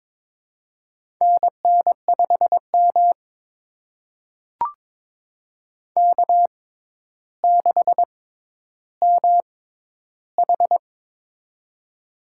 Morse code, heard as a keyed tone, about 23 wpm, spaced as for 11 wpm, 700 Hz. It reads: NN5M K6MH
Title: USA Call Signs - 4 Characters - 6x Character Spacing - 22wpm